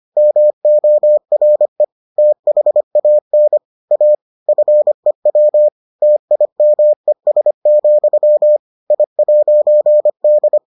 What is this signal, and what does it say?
Morse code, heard as a keyed tone, about 25 wpm, 600 Hz. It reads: MORE THAN A FEW TIMES, I'D